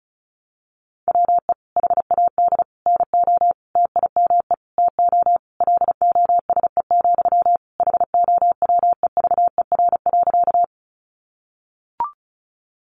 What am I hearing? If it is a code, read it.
Morse code, heard as a keyed tone, about 34 wpm, 700 Hz. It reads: WE HAD NO TIME TO LOSE, HOWEVER.
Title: Sentences from Top 1100 Words 35wpm